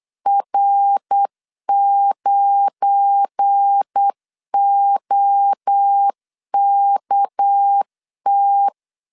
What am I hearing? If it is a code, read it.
Morse code, heard as a keyed tone, about 8 wpm, 800 Hz. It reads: R9OKT
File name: LEZIONI / 8 YJ45 / 8 YJ 8wpm 3313